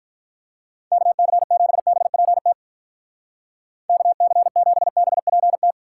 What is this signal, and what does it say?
Morse code, heard as a keyed tone, about 48 wpm, 700 Hz. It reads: KC7BPT KC7BPT